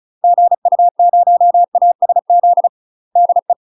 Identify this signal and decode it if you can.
Morse code, heard as a keyed tone, about 36 wpm, 700 Hz. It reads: GU0ASZ DE
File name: UK MASTER QSOs TRAINING / 35WPM / UK MASTER QSOs 0026